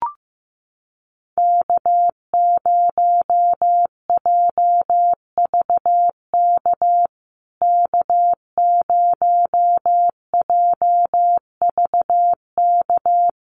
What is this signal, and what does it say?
Morse code, heard as a keyed tone, about 15 wpm, 700 Hz. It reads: K0JVK K0JVK
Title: USA Call Signs Repeated 3x 15wpm